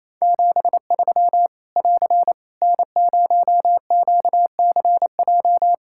Morse code, heard as a keyed tone, about 27 wpm, 700 Hz. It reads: 73 <AR> N0QCJ
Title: K1USN SST QSO Practice 28wpm